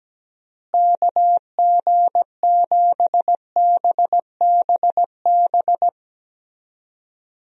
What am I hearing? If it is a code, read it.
Morse code, heard as a keyed tone, about 17 wpm, 700 Hz. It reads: KG7BBB
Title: USA Call Signs - 6 Characters - Repeated 2x - 17wpm